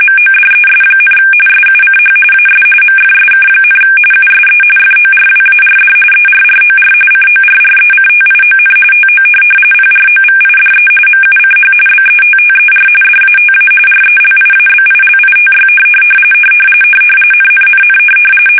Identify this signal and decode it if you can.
RTTY, about 45.45 baud. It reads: MODES, SUCH AS MFSK, USE FORWARD ERROR CORRECTION TO PROVIDE MUCH BETTER DATA RELIABILITY NNNN
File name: fsk850 / teletype-RTTY